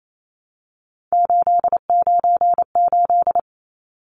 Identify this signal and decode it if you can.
Morse code, heard as a keyed tone, about 28 wpm, 700 Hz. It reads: 898
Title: Numbers - 3-Digit 28wpm